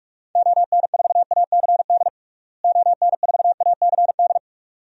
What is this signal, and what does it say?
Morse code, heard as a keyed tone, about 46 wpm, 700 Hz. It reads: ON4ACD ON4ACD
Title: DXCC-100 Call Signs - 6 Characters - Repeated 2x - 45wpm